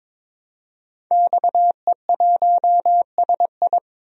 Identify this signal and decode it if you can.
Morse code, heard as a keyed tone, about 22 wpm, 700 Hz. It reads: XE1SI
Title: DXCC-100 Call Signs - 5 Characters - Repeated 3x - 22wpm